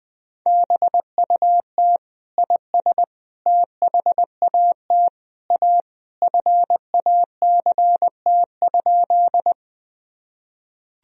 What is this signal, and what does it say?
Morse code, heard as a keyed tone, about 21 wpm, 700 Hz. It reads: BUT IS THAT A FACT?
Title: Sentences from Top 1000 Words - Review Part 3 - 20wpm